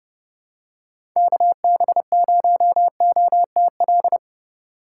Morse code, heard as a keyed tone, about 31 wpm, 700 Hz. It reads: KB0OTL
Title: USA Call Signs - 6 Characters - 3x Character Spacing - 30wpm